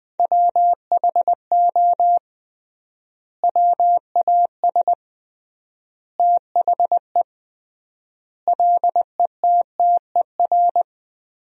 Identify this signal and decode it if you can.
Morse code, heard as a keyed tone, about 20 wpm, 700 Hz. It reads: WHO WAS THE LETTER
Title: 3x Word-Spacing - Sentences from Top 300 Words - 20wpm